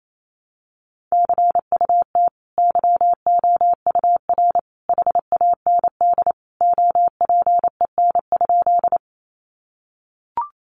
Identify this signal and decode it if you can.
Morse code, heard as a keyed tone, about 28 wpm, 700 Hz. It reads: CUT YOUR HAND OPEN?
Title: Sentences from Top 300 Words 28wpm